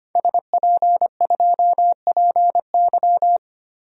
Morse code, heard as a keyed tone, about 25 wpm, 700 Hz. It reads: SP2PY
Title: DXCC-100 Call Signs - 5 Characters - Repeated 2x - 25wpm